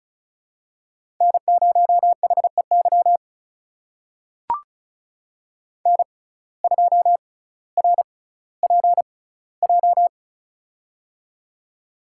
Morse code, heard as a keyed tone, about 36 wpm, spaced as for 18 wpm, 700 Hz. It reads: N0HEY N2RPJ
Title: USA Call Signs - 5 Characters - 6x Character Spacing - 35wpm